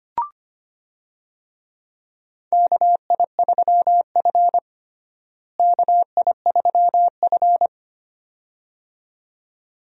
Morse code, heard as a keyed tone, about 25 wpm, 700 Hz. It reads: KI3F KI3F
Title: USA Call Signs - 4 Characters - Repeated 2x - 25wpm